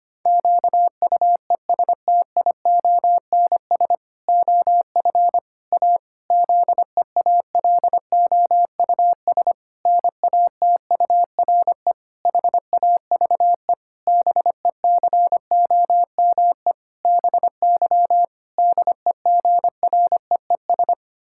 Morse code, heard as a keyed tone, about 25 wpm, 700 Hz. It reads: QUESTIONS OF A ZEALOUS NATURE HAVE BECOME BY DEGREES
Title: Pangram Challenge v2 - 25wpm